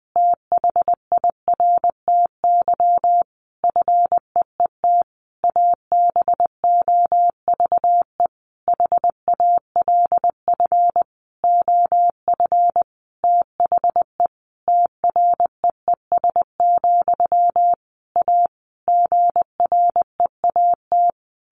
Morse code, heard as a keyed tone, about 19 wpm, 700 Hz. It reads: THIRTY FEET ABOVE HALF OF THE TREES, A GREAT
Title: A Sound Of Thunder 20wpm